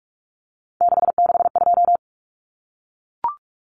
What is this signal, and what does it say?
Morse code, heard as a keyed tone, about 44 wpm, 700 Hz. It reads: 662
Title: Common Cut Numbers - 3-Digit 45wpm